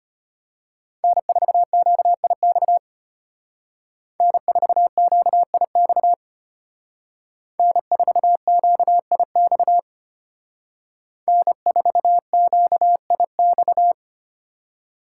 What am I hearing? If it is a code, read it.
Morse code, heard as a keyed tone, about 32 wpm, 700 Hz. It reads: N4QIX N4QIX N4QIX N4QIX